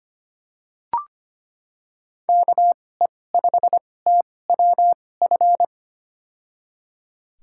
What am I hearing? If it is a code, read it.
Morse code, heard as a keyed tone, about 25 wpm, spaced as for 18 wpm, 700 Hz. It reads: KE5TWF
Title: USA Call Signs - 6 Characters - 2x Character Spacing - 25wpm